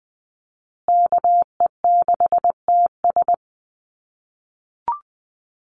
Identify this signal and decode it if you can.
Morse code, heard as a keyed tone, about 20 wpm, 700 Hz. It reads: KE6TS
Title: USA Call Signs Repeated 3x 20wpm